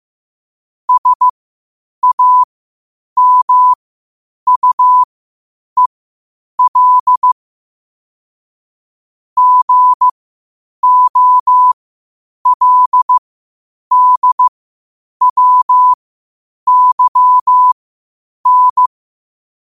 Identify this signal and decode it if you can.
Morse code, heard as a keyed tone, about 15 wpm, spaced as for 8 wpm, 1000 Hz. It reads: SAMUEL GOLDWYN
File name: qotdfiles / 8wpm